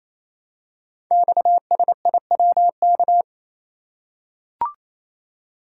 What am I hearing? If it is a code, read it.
Morse code, heard as a keyed tone, about 29 wpm, 700 Hz. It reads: XSIWK
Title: Code Groups - 5 Letters - 28wpm